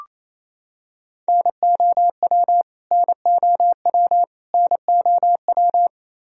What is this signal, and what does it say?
Morse code, heard as a keyed tone, about 28 wpm, 700 Hz. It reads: NOW NOW NOW
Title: Top 100 Words Repeated 28wpm